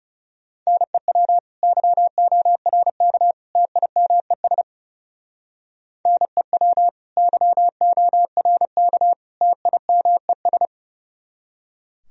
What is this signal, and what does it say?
Morse code, heard as a keyed tone, about 36 wpm, 700 Hz. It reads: NEW YORK TIMES NEW YORK TIMES
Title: Speed-Racing - Sets of 3 Words 30wpm